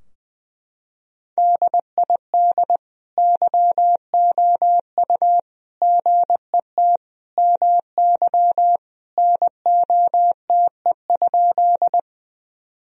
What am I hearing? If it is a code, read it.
Morse code, heard as a keyed tone, about 20 wpm, 700 Hz. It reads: DID YOU GET MY NOTE?
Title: Sentences from Top 1000 Words - Review Part 2 - 20wpm